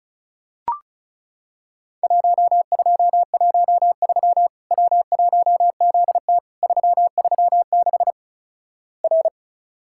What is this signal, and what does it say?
Morse code, heard as a keyed tone, about 36 wpm, 700 Hz. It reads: 1213 W1ZT 336 R
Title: Worked All Europe DX Contest - QTC Exchange - 35wpm